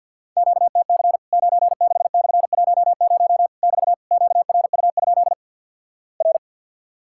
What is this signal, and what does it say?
Morse code, heard as a keyed tone, about 52 wpm, 700 Hz. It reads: QTC 96/10 <BT> QRU? R
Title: Worked All Europe DX Contest - QTC Exchange - 50wpm